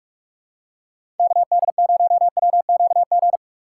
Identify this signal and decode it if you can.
Morse code, heard as a keyed tone, about 44 wpm, 700 Hz. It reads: KD0WQG
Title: USA Call Signs - 6 Characters - 4x Character Spacing - 45wpm